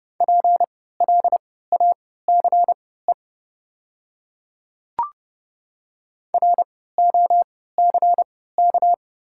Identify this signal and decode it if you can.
Morse code, heard as a keyed tone, about 30 wpm, spaced as for 17 wpm, 700 Hz. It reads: PLACE ROCK
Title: 3x Character Spacing - Top 500 Words - 30wpm